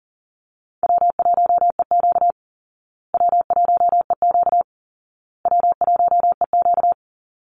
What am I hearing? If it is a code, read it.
Morse code, heard as a keyed tone, about 40 wpm, 700 Hz. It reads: W1EQ W1EQ W1EQ
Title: Easy Call Signs Repeated 3x - 40wpm